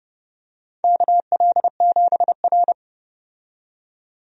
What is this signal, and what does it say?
Morse code, heard as a keyed tone, about 30 wpm, 700 Hz. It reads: KL7R
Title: USA Call Signs - 4 Characters - 30wpm